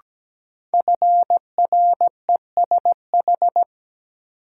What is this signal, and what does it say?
Morse code, heard as a keyed tone, about 17 wpm, 700 Hz. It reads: FRESH
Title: Sentences from Top 1200 Words - New Word Drill - 17wpm